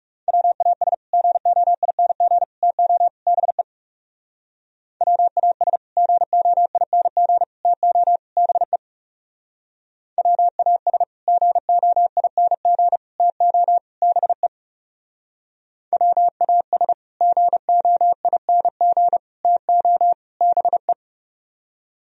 Morse code, heard as a keyed tone, about 37 wpm, 700 Hz. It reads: WAS GOING TO BE WAS GOING TO BE WAS GOING TO BE WAS GOING TO BE
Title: Speed-Racing - Sets of 4 Words 30wpm